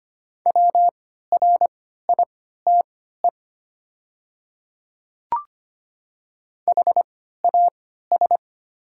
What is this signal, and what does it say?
Morse code, heard as a keyed tone, about 25 wpm, spaced as for 14 wpm, 700 Hz. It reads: WRITE HAS
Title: 3x Character Spacing - Top 100 Words - 25wpm